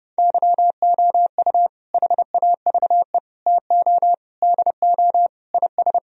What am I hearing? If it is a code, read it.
Morse code, heard as a keyed tone, about 29 wpm, 700 Hz. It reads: YOU HAVE TO DO IS